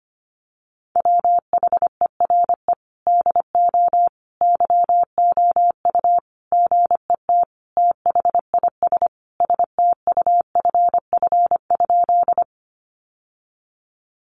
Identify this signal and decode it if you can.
Morse code, heard as a keyed tone, about 25 wpm, 700 Hz. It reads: WHERE DO YOU GET THIS STUFF?
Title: Sentences from Top 1400 Words 25wpm